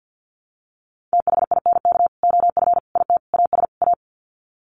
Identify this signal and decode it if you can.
Morse code, heard as a keyed tone, about 48 wpm, 700 Hz. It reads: THINK OF IT AS A
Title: Sets of 5 Words Encore2 50wpm